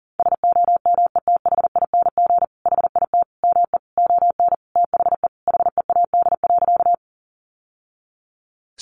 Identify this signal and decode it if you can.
Morse code, heard as a keyed tone, about 40 wpm, 700 Hz. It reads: SOMETHING HIT ME ON THE HEAD.
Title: Sentences from Top 1200 Words 40wpm